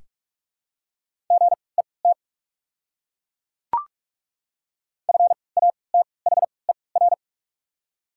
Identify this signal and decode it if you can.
Morse code, heard as a keyed tone, about 48 wpm, 690 Hz. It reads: G E T F A T H E R
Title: 3x Character Spacing - Top 200 Words - 45wpm